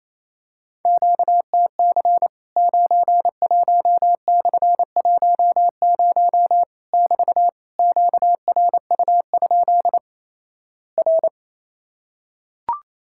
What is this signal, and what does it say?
Morse code, heard as a keyed tone, about 28 wpm, 700 Hz. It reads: QTC 91/10 <BT> QRU? R
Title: Worked All Europe DX Contest - QTC Exchange - 28wpm